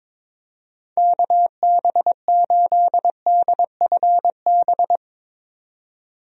Morse code, heard as a keyed tone, about 22 wpm, 700 Hz. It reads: KB8DFB
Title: USA Call Signs - 6 Characters - 22wpm